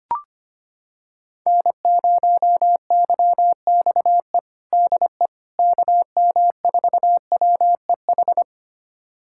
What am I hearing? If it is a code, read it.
Morse code, heard as a keyed tone, about 25 wpm, 700 Hz. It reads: N0YXE DE KM4WEH
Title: USA Callsign DE Callsign 25wpm